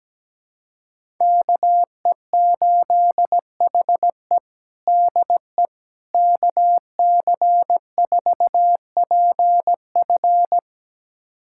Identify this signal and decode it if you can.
Morse code, heard as a keyed tone, about 17 wpm, 700 Hz. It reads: KE8HE DE KC4PF
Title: USA Callsign DE Callsign - 5 Characters - 17wpm